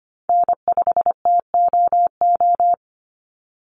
Morse code, heard as a keyed tone, about 25 wpm, 700 Hz. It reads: N5TOO
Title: Call Signs Repeated 2x - 25wpm